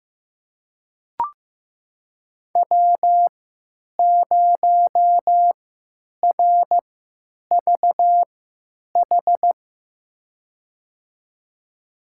Morse code, heard as a keyed tone, about 15 wpm, spaced as for 9 wpm, 700 Hz. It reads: W0RVH
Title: USA Call Signs - 5 Characters - 3x Character Spacing - 15wpm